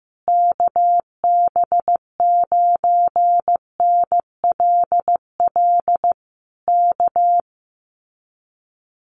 Morse code, heard as a keyed tone, about 15 wpm, 700 Hz. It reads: KB9NLL K
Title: Common QSO Phrases Encore 15wpm